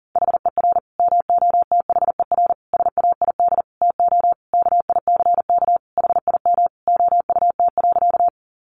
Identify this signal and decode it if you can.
Morse code, heard as a keyed tone, about 40 wpm, 700 Hz. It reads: HER MOTHER SAID TO KICK HIM OUT.
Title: Sentences from Top 1300 Words 40wpm